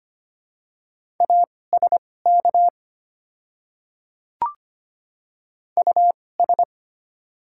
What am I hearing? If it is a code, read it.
Morse code, heard as a keyed tone, about 25 wpm, spaced as for 18 wpm, 690 Hz. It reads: ASK US
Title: 2x Character Spacing - Top 300 Words - 25wpm